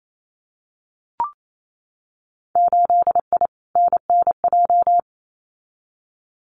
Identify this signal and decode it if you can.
Morse code, heard as a keyed tone, about 28 wpm, 700 Hz. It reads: 8I NNJ
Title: Winter Field Day Exchanges Encore 28wpm